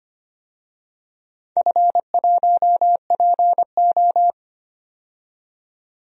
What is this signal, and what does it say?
Morse code, heard as a keyed tone, about 25 wpm, 700 Hz. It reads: F1PO
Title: DXCC-100 Call Signs - 4 Characters - 25wpm